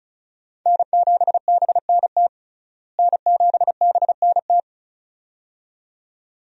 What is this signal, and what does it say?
Morse code, heard as a keyed tone, about 34 wpm, 700 Hz. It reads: N7BNT N7BNT
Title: USA Call Signs - 5 Characters - Repeated 2x - 35wpm